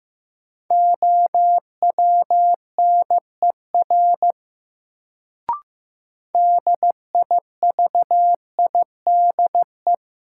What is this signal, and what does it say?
Morse code, heard as a keyed tone, about 15 wpm, 700 Hz. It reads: OWNER DIVIDE
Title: Sentences from Top 1600 Words - New Word Drill Alt - 15wpm